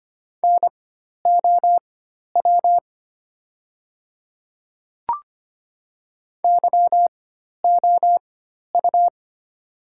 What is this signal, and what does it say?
Morse code, heard as a keyed tone, about 25 wpm, spaced as for 12 wpm, 700 Hz. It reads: NOW YOU
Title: 4x Character Spacing - Top 200 Words - 25wpm